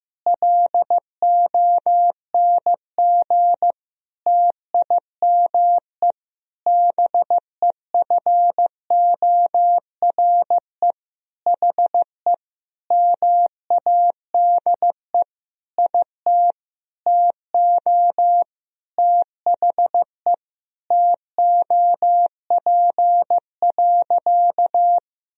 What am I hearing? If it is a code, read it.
Morse code, heard as a keyed tone, about 16 wpm, 700 Hz. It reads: LONG TIME BEFORE HE MADE IT TO THE TOP.
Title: Sentences from Top 1000 Words - Review Part 2 - 15wpm